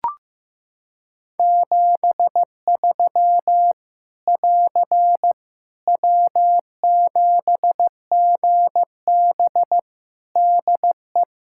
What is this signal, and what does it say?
Morse code, heard as a keyed tone, about 15 wpm, 700 Hz. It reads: 73 <AR> W7GB DE
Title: Practice QSO - Exchange 4 - Ending - 15wpm